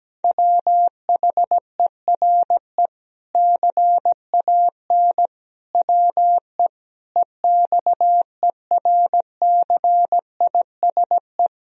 Morse code, heard as a keyed tone, about 17 wpm, 700 Hz. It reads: WHERE CAN WE EXERCISE
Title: Sentences from Top 1000 Words - Review Part 4 - 17wpm